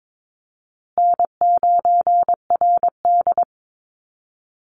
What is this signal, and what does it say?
Morse code, heard as a keyed tone, about 22 wpm, 700 Hz. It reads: N9RD
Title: Easy Call Signs Repeated 2x - 22wpm